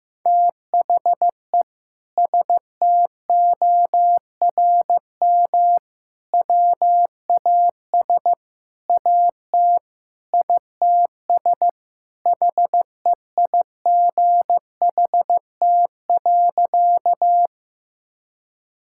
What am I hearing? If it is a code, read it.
Morse code, heard as a keyed tone, about 16 wpm, 700 Hz. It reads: THE STORM WAS AT ITS HEIGHT.
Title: Sentences from Top 1600 Words 15wpm